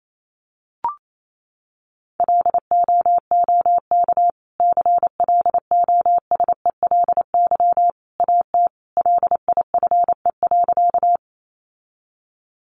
Morse code, heard as a keyed tone, about 28 wpm, 700 Hz. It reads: LOOK CLOSELY AT LIFE.
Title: Sentences from Top 1400 Words 28wpm